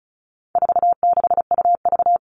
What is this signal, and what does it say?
Morse code, heard as a keyed tone, about 34 wpm, 700 Hz. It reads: 46UV
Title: All Cut Numbers - 4-Digit 35wpm